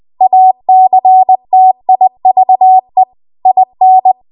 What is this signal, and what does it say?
Morse code, heard as a keyed tone, about 20 wpm, 750 Hz. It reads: ACTIVE IN
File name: Archive / 20 WPM / 20WPM